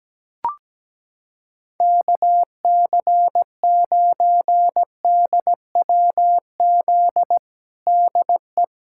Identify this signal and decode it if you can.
Morse code, heard as a keyed tone, about 17 wpm, 700 Hz. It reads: KC9DWZ DE
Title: USA Callsign DE Callsign - 6 Characters - 17wpm